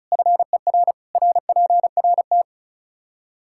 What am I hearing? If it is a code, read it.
Morse code, heard as a keyed tone, about 34 wpm, 700 Hz. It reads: FER RPRT
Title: Practice QSO - Exchange 2 - Info - 35wpm